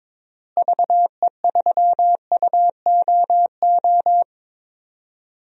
Morse code, heard as a keyed tone, about 22 wpm, 700 Hz. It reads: VE3UOO
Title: DXCC-100 Call Signs - 6 Characters - Repeated 3x - 22wpm